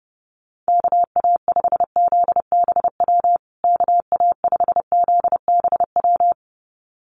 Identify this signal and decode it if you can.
Morse code, heard as a keyed tone, about 30 wpm, 700 Hz. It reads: KA5ZBW KA5ZBW
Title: USA Call Signs Repeated 2x 30wpm